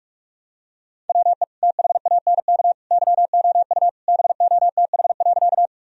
Morse code, heard as a keyed tone, about 41 wpm, 700 Hz. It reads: WE THANK YOU BOTH.
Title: Sentences from Top 1000 Words - Review Part 1 - 45wpm